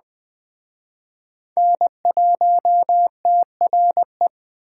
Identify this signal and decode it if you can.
Morse code, heard as a keyed tone, about 20 wpm, 700 Hz. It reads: N1TRE
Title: Speed Racing - USA 1x3 Call Signs 17wpm